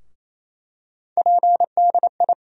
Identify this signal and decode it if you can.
Morse code, heard as a keyed tone, about 28 wpm, 700 Hz. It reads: PDI
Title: USA Call Sign Postfixes - 3 Letters 28wpm